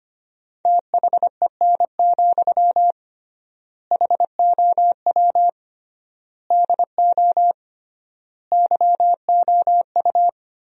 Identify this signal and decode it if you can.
Morse code, heard as a keyed tone, about 25 wpm, 700 Hz. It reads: THEN, HOW DO YOU
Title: 3x Word-Spacing - Sentences from Top 100 Words - 25wpm